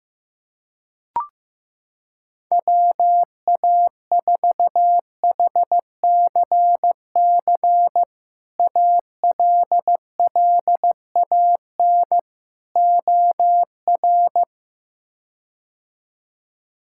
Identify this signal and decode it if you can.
Morse code, heard as a keyed tone, about 15 wpm, 700 Hz. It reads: WA4HCC ALLAN OR
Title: K1USN SST QSO Practice - 15wpm